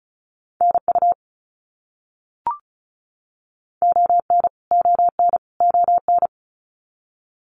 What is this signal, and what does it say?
Morse code, heard as a keyed tone, about 34 wpm, 700 Hz. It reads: NU ON ON ON
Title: Canadian Provinces and Territories - Repeated 3x - 35wpm